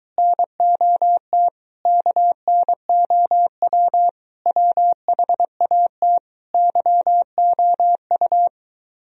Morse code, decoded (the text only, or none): NOT KNOW WHAT YOU